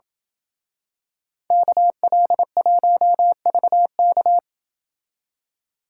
Morse code, heard as a keyed tone, about 26 wpm, 700 Hz. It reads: KL1VK